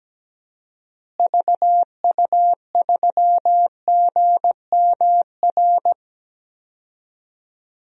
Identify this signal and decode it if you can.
Morse code, heard as a keyed tone, about 17 wpm, 700 Hz. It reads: VU3GMR
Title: DXCC-100 Call Signs - 6 Characters - Repeated 2x - 17wpm